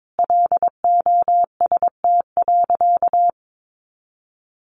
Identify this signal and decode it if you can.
Morse code, heard as a keyed tone, about 22 wpm, 700 Hz. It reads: LOST.
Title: Sentences from Top 1400 Words 22wpm